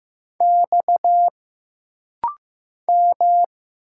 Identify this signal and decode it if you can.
Morse code, heard as a keyed tone, about 15 wpm, 700 Hz. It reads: X M
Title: Single Letters - Rapid-Fire 15wpm